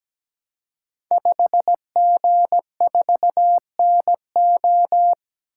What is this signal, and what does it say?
Morse code, heard as a keyed tone, about 17 wpm, 700 Hz. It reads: 5G4NO
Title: Code Groups - 5 Letter:Numbers - 17wpm